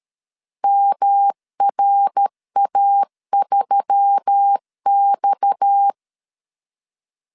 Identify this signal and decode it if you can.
Morse code, heard as a keyed tone, about 13 wpm, 800 Hz. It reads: MRA3X
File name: LEZIONI / 6 PKX0 / 6 PKX 12wpm 2024